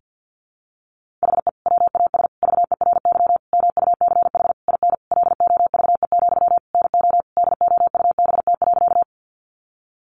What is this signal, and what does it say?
Morse code, heard as a keyed tone, about 50 wpm, 700 Hz. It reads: HE WAS VERY MUCH IN LOVE, NO DOUBT.
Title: Sentences from Top 1500 Words 50wpm